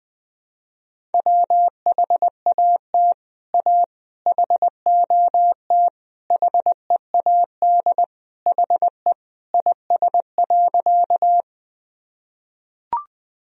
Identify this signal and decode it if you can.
Morse code, heard as a keyed tone, about 20 wpm, 700 Hz. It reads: WHAT A HOT HEAD HE IS.
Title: Sentences from Top 1000 Words - Review Part 3 - 20wpm